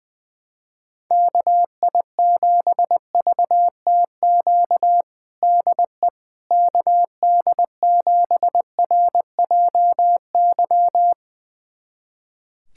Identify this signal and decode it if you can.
Morse code, heard as a keyed tone, about 21 wpm, 700 Hz. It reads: KI7VTQ DE KD7RJY
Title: USA Callsign DE Callsign - 6 Characters - 20wpm